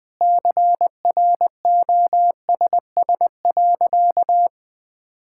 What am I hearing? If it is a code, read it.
Morse code, heard as a keyed tone, about 20 wpm, 700 Hz. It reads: CROSS.
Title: Sentences from Top 1000 Words - Review Part 6 - 20wpm